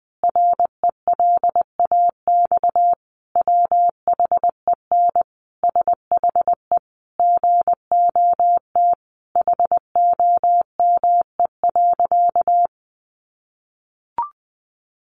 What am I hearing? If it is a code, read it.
Morse code, heard as a keyed tone, about 20 wpm, 700 Hz. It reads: RELAX WHEN SHE GOT HOME.
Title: Sentences from Top 1400 Words 20wpm